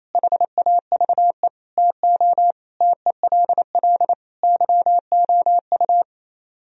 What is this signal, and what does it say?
Morse code, heard as a keyed tone, about 35 wpm, 700 Hz. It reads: HAVE TO TELL YOU